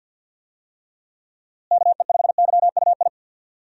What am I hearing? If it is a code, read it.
Morse code, heard as a keyed tone, about 47 wpm, 700 Hz. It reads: KE5YUI
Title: USA Call Signs - 6 Characters - Repeated 2x - 50wpm